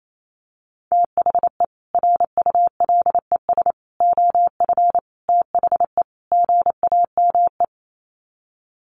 Morse code, heard as a keyed tone, about 28 wpm, 700 Hz. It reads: THE RULES OF THE GAME
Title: Sets of 5 Words Encore2 28wpm